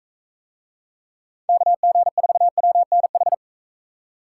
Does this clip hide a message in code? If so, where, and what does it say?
Yes: Morse code from 1.5 to 3.3 s, heard as a keyed tone, about 43 wpm, 700 Hz. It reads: KG4WNH